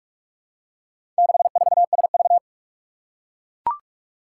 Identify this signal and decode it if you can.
Morse code, heard as a keyed tone, about 44 wpm, 700 Hz. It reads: 64SV
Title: Four Challenging Letter-Number Combinations 45wpm